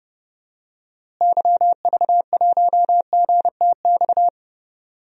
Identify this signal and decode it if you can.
Morse code, heard as a keyed tone, about 31 wpm, 700 Hz. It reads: YV1GTX
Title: DXCC-100 Call Signs - 6 Characters - Repeated 2x - 30wpm